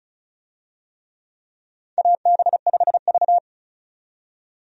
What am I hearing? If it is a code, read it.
Morse code, heard as a keyed tone, about 36 wpm, 700 Hz. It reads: AB5V
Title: USA Call Signs - 4 Characters - 35wpm